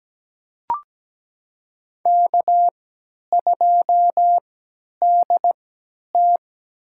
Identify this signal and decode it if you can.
Morse code, heard as a keyed tone, about 17 wpm, spaced as for 10 wpm, 700 Hz. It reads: K2DT
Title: USA Call Signs - 3x Character Spacing - 17wpm